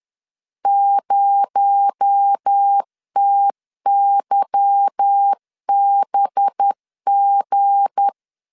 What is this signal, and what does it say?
Morse code, heard as a keyed tone, about 10 wpm, 800 Hz. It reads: 0TYBG
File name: LEZIONI / 10 SIGNS / 10D COMA 10wpm 3013